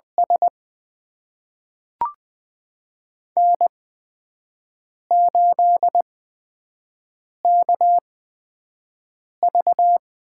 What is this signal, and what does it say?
Morse code, heard as a keyed tone, about 20 wpm, spaced as for 5 wpm, 700 Hz. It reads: S N8KV
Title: USA Call Signs - 8x Character Spacing - 20wpm